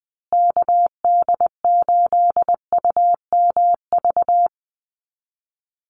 Morse code, heard as a keyed tone, about 20 wpm, 700 Hz. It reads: KD8UMV
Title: USA Call Signs Repeated 2x 20wpm